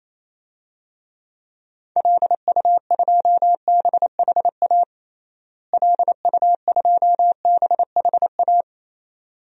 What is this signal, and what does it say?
Morse code, heard as a keyed tone, about 27 wpm, 700 Hz. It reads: LU2BHA LU2BHA
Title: DXCC-100 Call Signs - 6 Characters - Repeated 2x - 28wpm